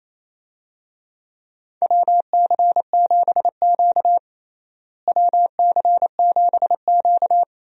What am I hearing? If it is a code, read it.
Morse code, heard as a keyed tone, about 28 wpm, 700 Hz. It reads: WC7Q WC7Q
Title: USA Call Signs - 4 Characters - Repeated 2x - 28wpm